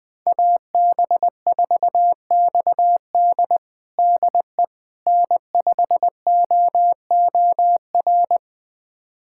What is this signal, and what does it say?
Morse code, heard as a keyed tone, about 21 wpm, 700 Hz. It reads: AB4XD DE N5OOR
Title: USA Callsign DE Callsign - 5 Characters - 20wpm